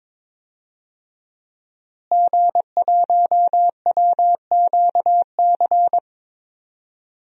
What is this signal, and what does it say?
Morse code, heard as a keyed tone, about 22 wpm, 700 Hz. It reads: G1WQC
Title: DXCC-100 Call Signs - 5 Characters - Repeated 2x - 22wpm